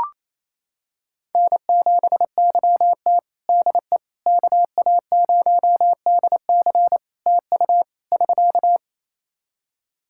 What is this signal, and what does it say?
Morse code, heard as a keyed tone, about 28 wpm, 700 Hz. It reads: N7YT DE KA0DC TU <SK>
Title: Practice QSO - Exchange 4 - Ending - 28wpm